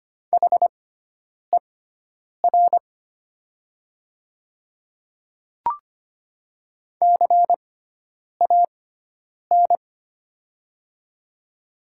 Morse code, heard as a keyed tone, about 25 wpm, spaced as for 9 wpm, 700 Hz. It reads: HER CAN
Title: 6x Character Spacing - Top 100 Words - 25wpm